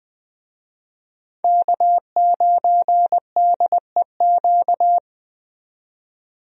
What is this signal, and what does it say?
Morse code, heard as a keyed tone, about 20 wpm, 700 Hz. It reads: K9DEQ
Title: USA Call Signs - 5 Characters - 20wpm